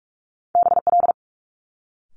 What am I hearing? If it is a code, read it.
Morse code, heard as a keyed tone, about 46 wpm, 700 Hz. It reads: BL
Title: Multi-letter Phonemes Quick Fire 45wpm